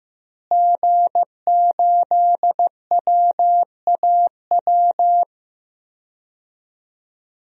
Morse code, heard as a keyed tone, about 15 wpm, 700 Hz. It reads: G8WAW
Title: DXCC-100 Call Signs - 5 Characters - Repeated 3x - 15wpm